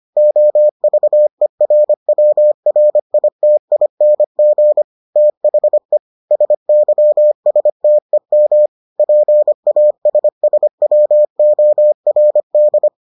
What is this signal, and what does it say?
Morse code, heard as a keyed tone, about 25 wpm, 600 Hz. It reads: OVERWRITING THE SYSTEM PASSWORD